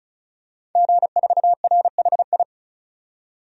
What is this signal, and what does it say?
Morse code, heard as a keyed tone, about 34 wpm, 700 Hz. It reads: G4RHI
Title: DXCC-100 Call Signs - 5 Characters - 35wpm